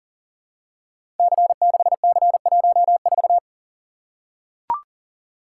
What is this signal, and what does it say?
Morse code, heard as a keyed tone, about 40 wpm, 690 Hz. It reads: C6C14
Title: Code Groups - 5 Letter:Numbers - 40wpm